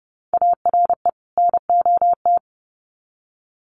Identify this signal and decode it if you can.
Morse code, heard as a keyed tone, about 30 wpm, 700 Hz. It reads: ARE NOT
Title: Sets of 2 Words 30wpm